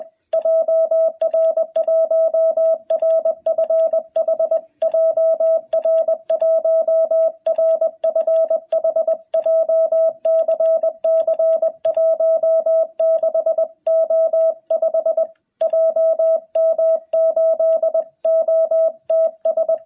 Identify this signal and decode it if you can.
Morse code, heard as a keyed tone, about 21 wpm, 650 Hz. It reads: JR1RFH JR1RFH JCC16O5 JM8OTS
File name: jh8cbh / 2021cw